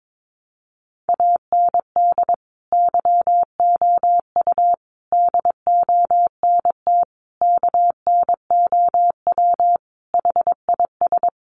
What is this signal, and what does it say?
Morse code, heard as a keyed tone, about 22 wpm, 700 Hz. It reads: AND YOU DONT KNOW HIS
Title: Sentences from Top 1000 Words 22wpm